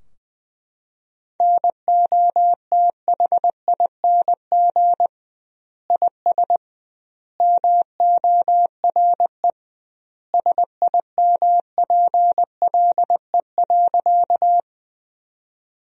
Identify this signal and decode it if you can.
Morse code, heard as a keyed tone, about 20 wpm, 700 Hz. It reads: NOTHING IS MORE SIMPLE.
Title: 2x Word-Spacing - Sentences from Top 500 Words - 20wpm